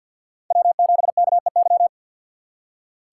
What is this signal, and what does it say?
Morse code, heard as a keyed tone, about 47 wpm, 690 Hz. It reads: W7KEY
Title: USA Call Signs - 5 Characters - 3x Character Spacing - 50wpm